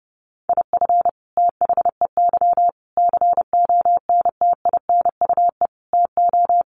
Morse code, heard as a keyed tone, about 30 wpm, 700 Hz. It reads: IF THEY CONTINUE TO
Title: Sentences from Top 1400 Words 30wpm